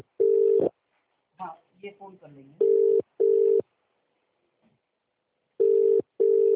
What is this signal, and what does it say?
Ringback tone (UK), 400 + 450 Hz, with speech.